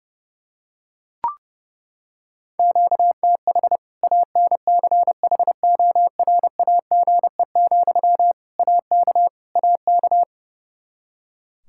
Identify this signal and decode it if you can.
Morse code, heard as a keyed tone, about 32 wpm, 700 Hz. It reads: QTH ANCHORAGE, AK AK
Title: Practice QSO - Exchange 1 - Beginning - 30wpm